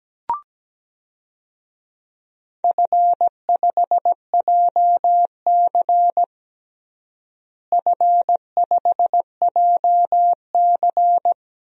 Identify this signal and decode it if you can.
Morse code, heard as a keyed tone, about 17 wpm, 700 Hz. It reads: F5JC F5JC
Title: DXCC-100 Call Signs - 4 Characters - Repeated 3x - 17wpm